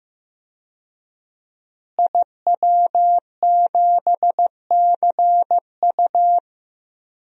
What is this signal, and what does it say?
Morse code, heard as a keyed tone, about 15 wpm, 700 Hz. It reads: IW7CU
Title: DXCC-100 Call Signs - 5 Characters - Repeated 2x - 15wpm